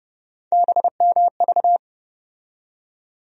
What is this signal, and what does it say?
Morse code, heard as a keyed tone, about 30 wpm, 700 Hz. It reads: BMV
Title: USA Call Sign Postfixes - 3 Letters 30wpm